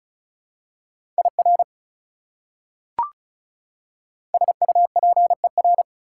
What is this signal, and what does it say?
Morse code, heard as a keyed tone, about 36 wpm, 700 Hz. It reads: IR SUPER
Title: Prefixes and Suffixes 35wpm